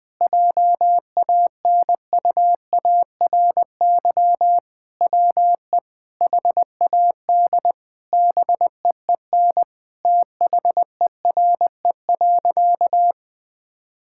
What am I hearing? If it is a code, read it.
Morse code, heard as a keyed tone, about 20 wpm, 700 Hz. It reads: JANUARY WE HAD BEEN THERE.
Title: Sentences with 4-Word Phrases - Spoken Beginning - 20wpm